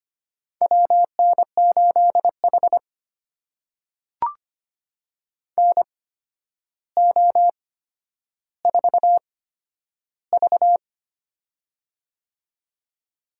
Morse code, heard as a keyed tone, about 25 wpm, spaced as for 11 wpm, 700 Hz. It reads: WN8H NO4V
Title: USA Call Signs - 4 Characters - 8x Character Spacing - 25wpm